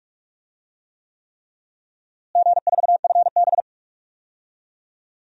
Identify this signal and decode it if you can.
Morse code, heard as a keyed tone, about 46 wpm, 700 Hz. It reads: G4FB
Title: DXCC-100 Call Signs - 4 Characters - Repeated 3x - 45wpm